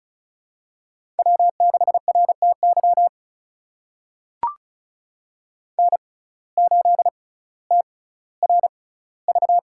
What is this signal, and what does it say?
Morse code, heard as a keyed tone, about 36 wpm, spaced as for 18 wpm, 700 Hz. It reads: W6RTY N8TRV
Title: USA Call Signs - 5 Characters - 6x Character Spacing - 35wpm